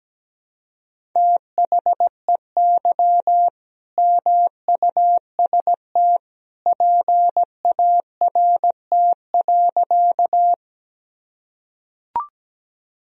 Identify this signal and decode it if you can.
Morse code, heard as a keyed tone, about 17 wpm, 700 Hz. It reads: THEY MUST PART.
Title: Sentences from Top 1000 Words - Review Part 1 - 17wpm